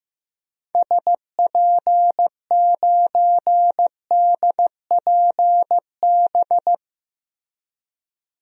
Morse code, heard as a keyed tone, about 15 wpm, 700 Hz. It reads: SP9DPB
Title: DXCC-100 Call Signs - 6 Characters - Repeated 3x - 15wpm